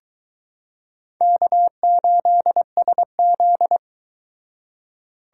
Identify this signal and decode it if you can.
Morse code, heard as a keyed tone, about 22 wpm, 700 Hz. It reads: K8SZ